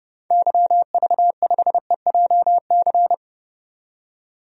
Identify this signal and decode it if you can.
Morse code, heard as a keyed tone, about 32 wpm, 700 Hz. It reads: YV5EJC
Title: DXCC-100 Call Signs - 6 Characters - 30wpm